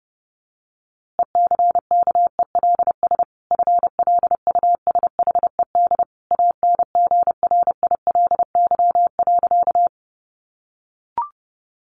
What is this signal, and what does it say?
Morse code, heard as a keyed tone, about 30 wpm, 700 Hz. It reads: ECKELS FLUSHED ANGRILY.
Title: A Sound Of Thunder 30wpm